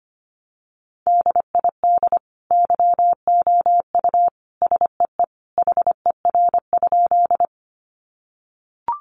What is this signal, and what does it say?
Morse code, heard as a keyed tone, about 25 wpm, 700 Hz. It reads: DID YOU SEE HER?
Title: Sentences from Top 100 Words 25wpm